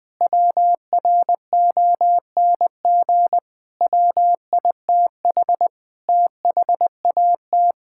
Morse code, heard as a keyed tone, about 20 wpm, 700 Hz. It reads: WRONG WITH THAT
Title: Sentences with 7-Word Phrases - Spoken Beginning - 20wpm